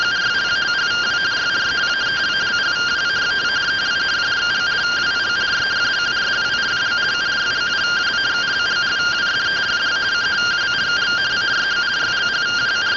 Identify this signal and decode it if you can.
Two-tone teleprinter signal, about 170 Hz shift.